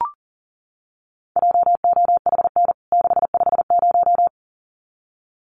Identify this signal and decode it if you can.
Morse code, heard as a keyed tone, about 40 wpm, 700 Hz. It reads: JOHN 650
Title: CWT Exchange 40wpm